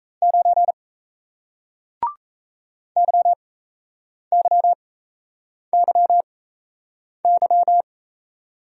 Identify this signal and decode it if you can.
Morse code, heard as a keyed tone, about 37 wpm, spaced as for 8 wpm, 700 Hz. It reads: 9 YYYY